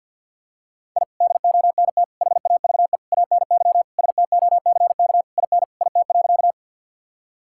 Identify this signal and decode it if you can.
Morse code, heard as a keyed tone, about 46 wpm, 700 Hz. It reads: I DONT HAVE ANY STOCK IN IT.
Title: Sentences from Top 1000 Words - Review Part 5 - 50wpm